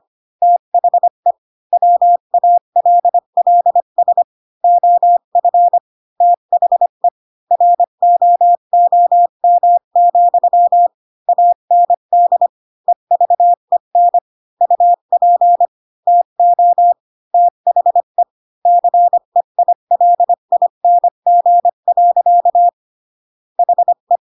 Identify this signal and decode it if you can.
Morse code, heard as a keyed tone, about 25 wpm, 700 Hz. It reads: THE WALLS OF THE ROOM, AND EVEN UP TO THE CEILING. HE
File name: Pinocchio / Pinocchio ch1 20 25 ik1hge